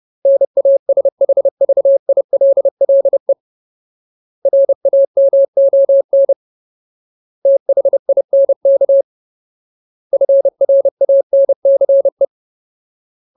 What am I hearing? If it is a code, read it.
Morse code, heard as a keyed tone, about 30 wpm, 550 Hz. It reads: NASHVILLE RAMON THINK FRANCE
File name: Ultimate Word List / 30WPM Ultimate